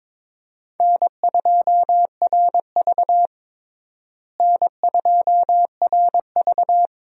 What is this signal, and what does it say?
Morse code, heard as a keyed tone, about 22 wpm, 700 Hz. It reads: N2RV N2RV
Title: USA Call Signs - 4 Characters - Repeated 3x - 22wpm